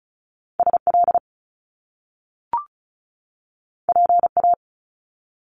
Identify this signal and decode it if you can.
Morse code, heard as a keyed tone, about 36 wpm, 700 Hz. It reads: SL PA
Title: Top Two Letter Combinations in Words 35wpm